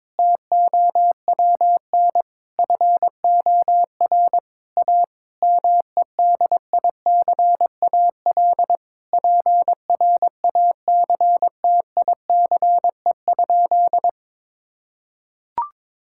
Morse code, heard as a keyed tone, about 22 wpm, 700 Hz. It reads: TOWN FOR A MEDICAL PRACTICE?
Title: Sentences from Top 1000 Words - Review Part 5 - 22wpm